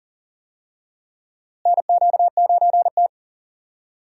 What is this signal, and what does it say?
Morse code, heard as a keyed tone, about 40 wpm, 700 Hz. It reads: NQ9T
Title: USA Call Signs - 4 Characters - Repeated 2x - 40wpm